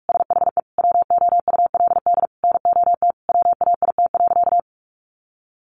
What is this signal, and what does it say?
Morse code, heard as a keyed tone, about 46 wpm, 700 Hz. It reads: SHE WOULD NOT WAIT.
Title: Sentences from Top 1400 Words 45wpm